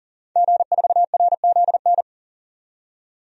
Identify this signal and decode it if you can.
Morse code, heard as a keyed tone, about 40 wpm, 700 Hz. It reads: G4RZN
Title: DXCC-100 Call Signs - 5 Characters - 40wpm